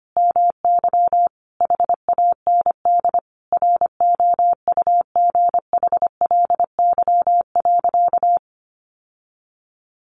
Morse code, heard as a keyed tone, about 25 wpm, 700 Hz. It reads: MY HAND ROUGHLY.
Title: Sentences from Top 1500 Words 25wpm